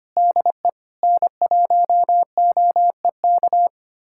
Morse code, heard as a keyed tone, about 25 wpm, 700 Hz. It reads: DE N1OEK
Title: USA Callsign DE Callsign - 5 Characters - 25wpm